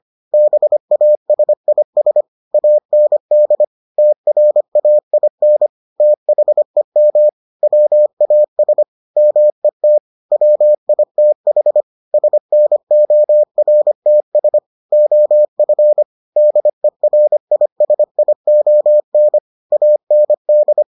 Morse code, heard as a keyed tone, about 25 wpm, 600 Hz. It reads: BASIS AND TRAIN THEM WAS MET WITH SNORTS OF DERISION AND